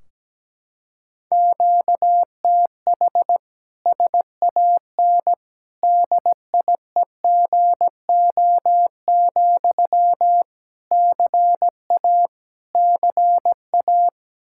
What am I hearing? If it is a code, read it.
Morse code, heard as a keyed tone, about 17 wpm, 700 Hz. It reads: QTH SAN DIEGO, CA CA
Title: Practice QSO - Exchange 1 - Beginning - 17wpm